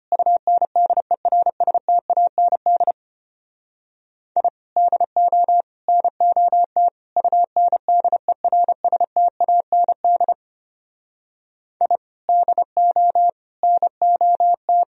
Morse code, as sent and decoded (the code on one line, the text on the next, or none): ..- -. -.. .-. ... - .- -. -.. / .. / -.. --- / -. --- - / ..- -. -.. .-. ... - .- -. -.. / .. / -.. --- / -. --- -
UNDERSTAND I DO NOT UNDERSTAND I DO NOT